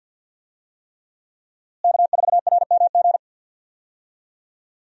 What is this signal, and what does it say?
Morse code, heard as a keyed tone, about 49 wpm, 700 Hz. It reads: K4RMG
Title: USA Call Signs - 5 Characters - Repeated 3x - 50wpm